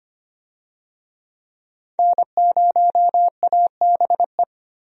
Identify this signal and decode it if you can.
Morse code, heard as a keyed tone, about 25 wpm, 700 Hz. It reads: N0ABE
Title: USA Call Signs - 5 Characters - 25wpm